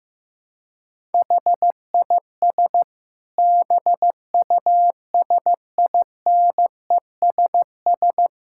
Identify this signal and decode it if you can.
Morse code, heard as a keyed tone, about 15 wpm, 700 Hz. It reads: HIS BUSINESS
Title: Sentences from Top 1000 Words - Review Part 6 - 15wpm